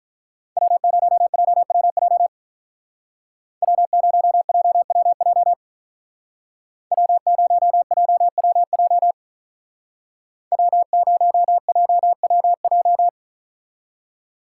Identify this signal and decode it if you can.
Morse code, heard as a keyed tone, about 41 wpm, 700 Hz. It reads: W0JWJ W0JWJ W0JWJ W0JWJ
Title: Speed Racing - USA 1x3 Call Signs 35wpm